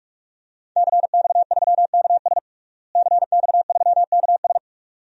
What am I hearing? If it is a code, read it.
Morse code, heard as a keyed tone, about 46 wpm, 700 Hz. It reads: CX3KS CX3KS
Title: DXCC-100 Call Signs - 5 Characters - Repeated 2x - 45wpm